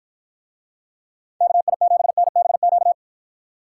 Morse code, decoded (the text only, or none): KI7NBQ